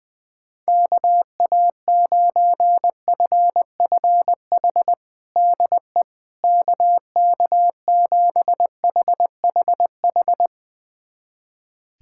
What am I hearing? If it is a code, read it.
Morse code, heard as a keyed tone, about 21 wpm, 700 Hz. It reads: KA9FFH DE KK7HHH
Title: USA Callsign DE Callsign - 6 Characters - 20wpm